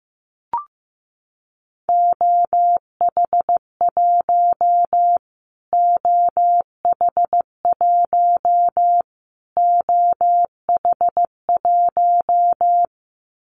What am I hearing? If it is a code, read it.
Morse code, heard as a keyed tone, about 15 wpm, 700 Hz. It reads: OH1 OH1 OH1
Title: Callsign Prefixes for Top 25 DXCC - Repeated 3x - 15wpm